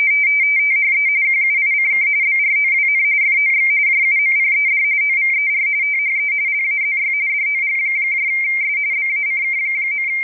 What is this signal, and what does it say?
RTTY, about 45.45 baud. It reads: THE QUICK BROWN FOX JUMPED OVER THE LAZY DOGS BACK.
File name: rtty / websdr recording 14086.0kHz